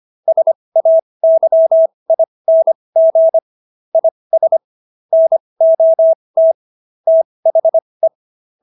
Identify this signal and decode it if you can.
Morse code, heard as a keyed tone, about 25 wpm, spaced as for 20 wpm, 650 Hz. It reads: SAYING IS NOT THE